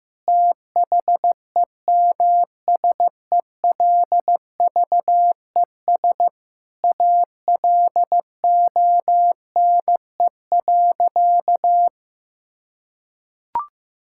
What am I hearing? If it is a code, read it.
Morse code, heard as a keyed tone, about 15 wpm, 700 Hz. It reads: THEMSELVES ALONE.
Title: Sentences from Top 1000 Words - Review Part 5 - 15wpm